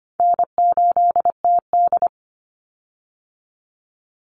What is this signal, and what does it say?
Morse code, heard as a keyed tone, about 25 wpm, 700 Hz. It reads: N8TD
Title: Easy Call Signs Repeated 3x - 25wpm